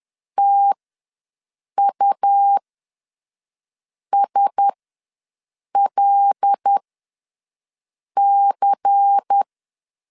Morse code, heard as a keyed tone, about 10 wpm, spaced as for 5 wpm, 800 Hz. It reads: TUSLC
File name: LEZIONI / 5 LBVC / 5 LBVC ws 10wpm 3733